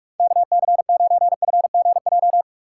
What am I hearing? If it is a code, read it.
Morse code, heard as a keyed tone, about 46 wpm, 700 Hz. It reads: KC9FGJ
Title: Speed Racing - USA Call Signs 45wpm